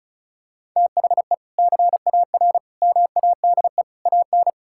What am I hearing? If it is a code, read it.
Morse code, heard as a keyed tone, about 36 wpm, 700 Hz. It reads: THE CAR MADE AN
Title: Sentences from Top 1000 Words - Review Part 6 - 35wpm